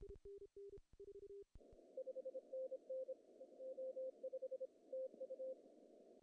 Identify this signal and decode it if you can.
Morse code, heard as a keyed tone, about 26 wpm, 520 Hz. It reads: 5NN O5 TU
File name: CW / cqww-cw-k5tr